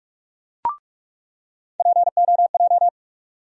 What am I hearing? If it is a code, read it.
Morse code, heard as a keyed tone, about 43 wpm, 700 Hz. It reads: POJ
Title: USA Call Sign Postfixes - 3 Letters 45wpm